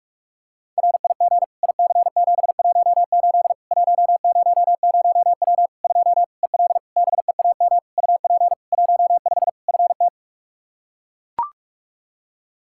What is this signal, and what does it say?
Morse code, heard as a keyed tone, about 41 wpm, 700 Hz. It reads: RIG IC718 100W 2 EL BEAM UP 15 FT
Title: Practice QSO - Exchange 2 - Info - 45wpm